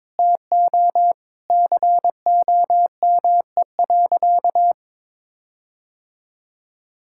Morse code, heard as a keyed tone, about 21 wpm, 700 Hz. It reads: TO COME.
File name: Sentences with 7-Word Phrases - 22wpm